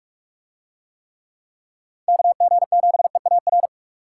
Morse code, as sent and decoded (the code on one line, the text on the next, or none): -.- --. --... .- .-.
KG7EAR